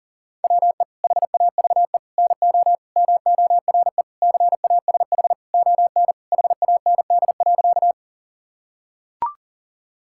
Morse code, heard as a keyed tone, about 40 wpm, 700 Hz. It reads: WE HAVE NO MORE CASH ON HAND.
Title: Sentences from Top 1000 Words - Review Part 5 - 40wpm